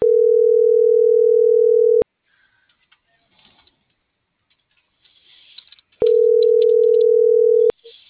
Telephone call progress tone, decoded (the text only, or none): ringback tone